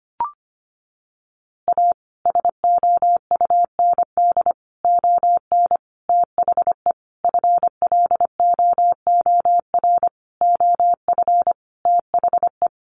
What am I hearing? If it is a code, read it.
Morse code, heard as a keyed tone, about 25 wpm, 700 Hz. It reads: A SOUND ON THE FLOOR OF THE
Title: A Sound Of Thunder 25wpm